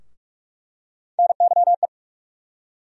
Morse code, heard as a keyed tone, about 46 wpm, 700 Hz. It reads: NYE
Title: USA Call Sign Postfixes - 3 Letters 45wpm